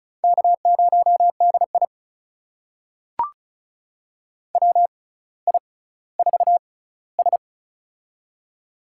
Morse code, heard as a keyed tone, about 36 wpm, spaced as for 18 wpm, 700 Hz. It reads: K0DI WI4S
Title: USA Call Signs - 4 Characters - 6x Character Spacing - 35wpm